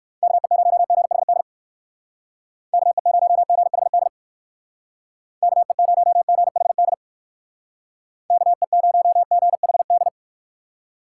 Morse code, heard as a keyed tone, about 55 wpm, 700 Hz. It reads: KE0GHD KE0GHD KE0GHD KE0GHD
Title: Speed Racing - USA 2x3 Call Signs 45wpm